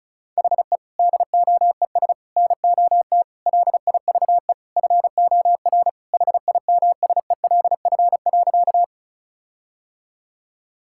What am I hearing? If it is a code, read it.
Morse code, heard as a keyed tone, about 34 wpm, 700 Hz. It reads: HE DOES NOT LIVE FOR HIMSELF.
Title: Sentences from Top 1600 Words 35wpm